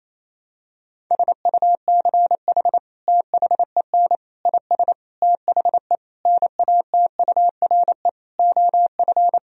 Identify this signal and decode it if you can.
Morse code, heard as a keyed tone, about 28 wpm, 700 Hz. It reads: SUCH THEN IS THE NATURE OF
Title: Sentences from Top 1000 Words - Review Part 4 - 28wpm